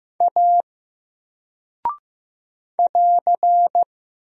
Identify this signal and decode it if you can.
Morse code, heard as a keyed tone, about 15 wpm, 700 Hz. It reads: A <AR>
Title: Single Character (Letter:Number:Punctuation:Prosign) 15wpm